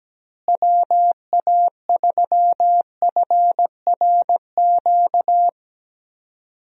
Morse code, heard as a keyed tone, about 17 wpm, 700 Hz. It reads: WA3FRQ
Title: USA Call Signs - 6 Characters - Repeated 2x - 17wpm